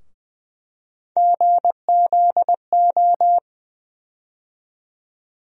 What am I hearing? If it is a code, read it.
Morse code, heard as a keyed tone, about 20 wpm, 700 Hz. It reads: GZO
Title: USA Call Sign Postfixes - 3 Letters 20wpm